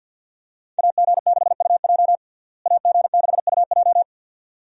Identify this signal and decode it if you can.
Morse code, heard as a keyed tone, about 47 wpm, 700 Hz. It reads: AG6UJ AG6UJ
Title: USA Call Signs - 5 Characters - Repeated 2x - 50wpm